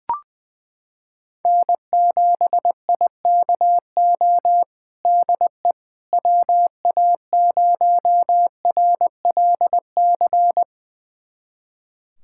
Morse code, heard as a keyed tone, about 20 wpm, 700 Hz. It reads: N7IKO DE WA0RLC
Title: USA Callsign DE Callsign 20wpm